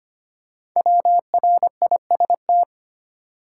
Morse code, heard as a keyed tone, about 25 wpm, 700 Hz. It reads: WRIST
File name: 5 Letter Words - First 3 Letters Given - 25wpm